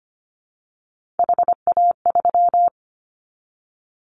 Morse code, heard as a keyed tone, about 25 wpm, 700 Hz. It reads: HA3
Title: Callsign Prefixes for Top 50 DXCC - 25wpm